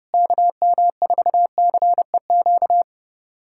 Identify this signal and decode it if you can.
Morse code, heard as a keyed tone, about 32 wpm, 700 Hz. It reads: KM4CEQ
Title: USA Callsign DE Callsign - 6 Characters - 30wpm